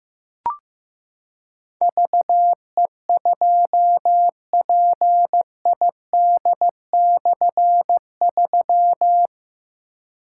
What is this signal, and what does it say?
Morse code, heard as a keyed tone, about 15 wpm, 700 Hz. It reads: VE2PID/3
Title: K1USN SST QSO Practice - 15wpm